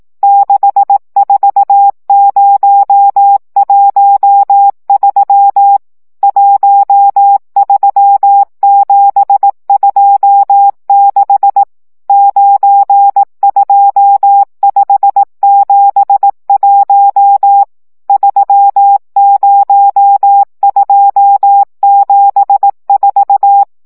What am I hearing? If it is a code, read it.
Morse code, heard as a keyed tone, about 18 wpm, 800 Hz. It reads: 64013 13726 92571 30274
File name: morze vizsga